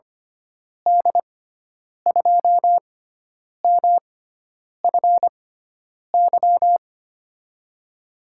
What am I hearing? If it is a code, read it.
Morse code, heard as a keyed tone, about 25 wpm, spaced as for 9 wpm, 700 Hz. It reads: D2MFY
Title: USA Call Signs - 6 Characters - 6x Character Spacing - 25wpm